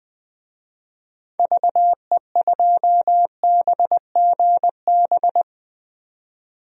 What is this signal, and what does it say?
Morse code, heard as a keyed tone, about 21 wpm, 700 Hz. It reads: VE2BGB
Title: DXCC-100 Call Signs - 6 Characters - 20wpm